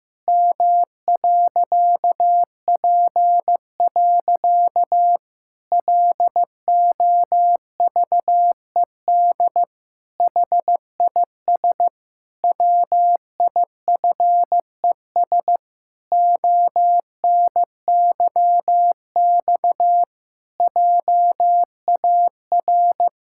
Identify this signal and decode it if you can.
Morse code, heard as a keyed tone, about 15 wpm, 700 Hz. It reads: M.P. LOVED HIS WIFES ONYX JAR
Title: Pangram Challenge v2 - 15wpm